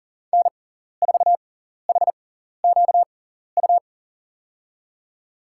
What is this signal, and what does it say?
Morse code, heard as a keyed tone, about 37 wpm, spaced as for 14 wpm, 700 Hz. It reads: N4HQU
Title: USA Call Signs - 6x Character Spacing - 40wpm